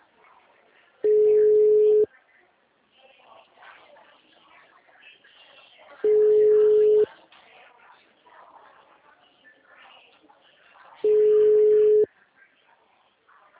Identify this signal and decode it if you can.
Ringback tone (Europe, 425 Hz), with speech.